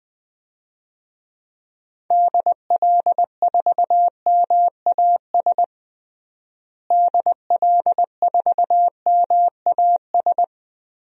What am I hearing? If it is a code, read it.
Morse code, heard as a keyed tone, about 20 wpm, 700 Hz. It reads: DL4MAS DL4MAS
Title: DXCC-100 Call Signs - 6 Characters - Repeated 3x - 20wpm